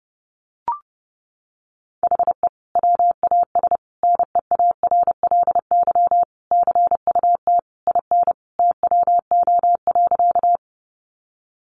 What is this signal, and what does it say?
Morse code, heard as a keyed tone, about 30 wpm, 700 Hz. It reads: HE WAS NEARLY CUT IN TWO.
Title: Sentences from Top 1300 Words 30wpm